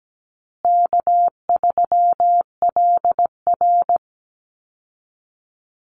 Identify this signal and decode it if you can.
Morse code, heard as a keyed tone, about 17 wpm, 700 Hz. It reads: K3LR
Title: Call Signs Repeated 2x - 17wpm